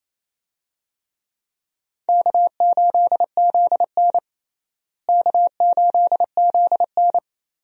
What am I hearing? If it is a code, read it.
Morse code, heard as a keyed tone, about 28 wpm, 700 Hz. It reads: K8ZN K8ZN
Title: USA Call Signs - 4 Characters - Repeated 3x - 28wpm